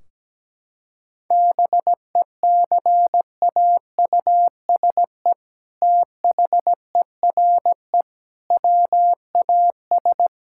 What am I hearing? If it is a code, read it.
Morse code, heard as a keyed tone, about 17 wpm, 700 Hz. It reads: BECAUSE THERE WAS
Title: Sentences from Top 1600 Words 17wpm